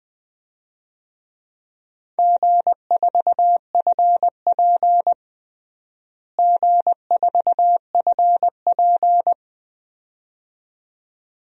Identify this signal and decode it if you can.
Morse code, heard as a keyed tone, about 20 wpm, 700 Hz. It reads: G4FP G4FP
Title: DXCC-100 Call Signs - 4 Characters - Repeated 2x - 20wpm